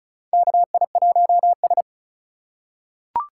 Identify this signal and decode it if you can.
Morse code, heard as a keyed tone, about 36 wpm, 700 Hz. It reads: KI1S
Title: USA Call Signs - 3x Character Spacing - 35wpm